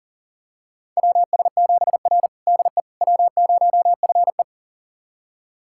Morse code, heard as a keyed tone, about 40 wpm, 700 Hz. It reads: WS7R DE W0FE
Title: USA Callsign DE Callsign - 4 Characters - 40wpm